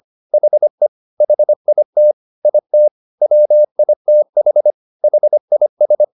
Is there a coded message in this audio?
Yes: Morse code from 0.3 to 6.0 s, heard as a keyed tone, about 25 wpm, 600 Hz. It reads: HE HIT IT WITH HIS